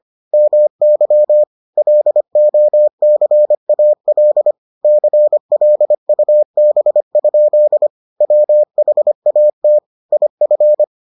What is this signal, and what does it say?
Morse code, heard as a keyed tone, about 25 wpm, 600 Hz. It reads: MY LOCAL CLUB? WHAT IF